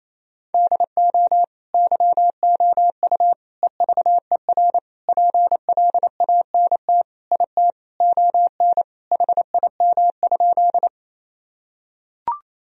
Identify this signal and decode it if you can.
Morse code, heard as a keyed tone, about 28 wpm, 700 Hz. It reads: DO YOU EVER PLANT IT ON HIM?
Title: Sentences from Top 1000 Words - Review Part 1 - 28wpm